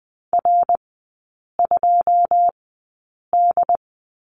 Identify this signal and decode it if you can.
Morse code, heard as a keyed tone, about 20 wpm, spaced as for 8 wpm, 700 Hz. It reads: R2D
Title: 093 Lesson 16e - Letter c - Three Characters 20wpm